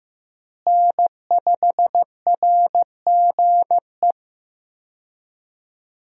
Morse code, heard as a keyed tone, about 15 wpm, 700 Hz. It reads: N5RGE
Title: USA Call Signs - 5 Characters - Repeated 2x - 15wpm